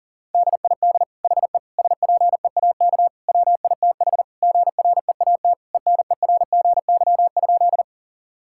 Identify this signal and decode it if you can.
Morse code, heard as a keyed tone, about 40 wpm, 700 Hz. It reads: DID HE SPEAK WITH GREAT ENERGY?
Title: Sentences from Top 1000 Words - Review Part 6 - 40wpm